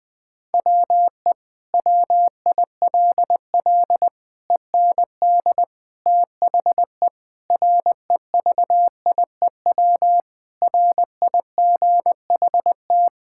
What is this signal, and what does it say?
Morse code, heard as a keyed tone, about 20 wpm, 700 Hz. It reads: WE WILL END THE REVIEW RIGHT
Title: Sentences from Top 1000 Words - Review Part 6 - 20wpm